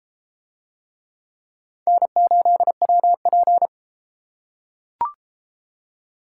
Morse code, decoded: N8WP

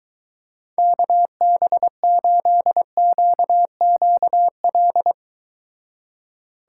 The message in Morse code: KB8QQL

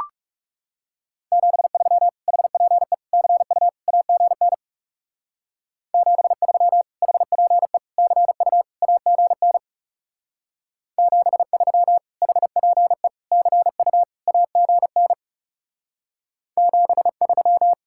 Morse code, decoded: 73 HPE CU AGN 73 HPE CU AGN 73 HPE CU AGN 73

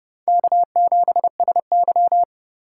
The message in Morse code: K7SY

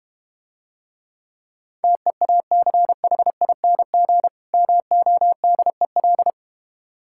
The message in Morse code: TEACHING MODEL

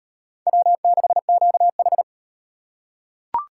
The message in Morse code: W6QH

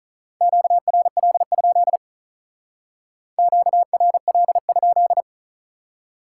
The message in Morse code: QRL? QRL?

